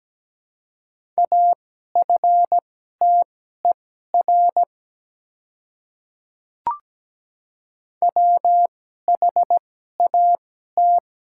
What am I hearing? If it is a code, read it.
Morse code, heard as a keyed tone, about 17 wpm, spaced as for 12 wpm, 700 Hz. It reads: AFTER WHAT
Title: 2x Character Spacing - Top 300 Words - 17wpm